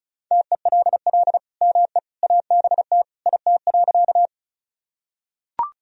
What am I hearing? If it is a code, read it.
Morse code, heard as a keyed tone, about 34 wpm, 700 Hz. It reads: TELL ME ABT IT.
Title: Sentences from Top 200 Words and Abbreviations 35wpm